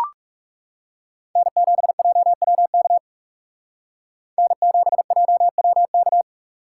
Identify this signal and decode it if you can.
Morse code, heard as a keyed tone, about 38 wpm, 700 Hz. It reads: N7JWK N7JWK